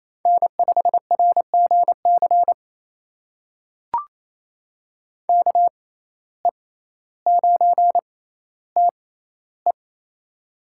Morse code, heard as a keyed tone, about 28 wpm, spaced as for 14 wpm, 700 Hz. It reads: N5RGC KE9TE
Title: USA Call Signs - 5 Characters - 6x Character Spacing - 28wpm